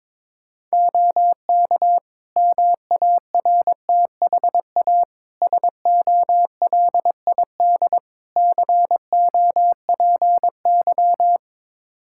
Morse code, heard as a keyed tone, about 22 wpm, 700 Hz. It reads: OK MARTHA SOLID COPY
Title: Practice QSO - Exchange 3 - More Info - 22wpm